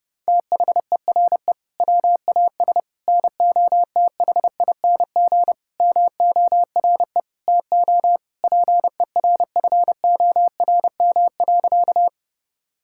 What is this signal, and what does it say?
Morse code, heard as a keyed tone, about 32 wpm, 700 Hz. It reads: THERE WAS NOTHING MORE TO PERFORM.
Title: Sentences from Top 1000 Words - Review Part 5 - 30wpm